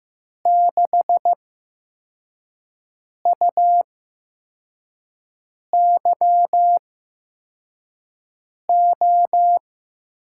Morse code, heard as a keyed tone, about 15 wpm, spaced as for 4 wpm, 700 Hz. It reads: 6UYO
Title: USA Call Signs - 6 Characters - 8x Character Spacing - 15wpm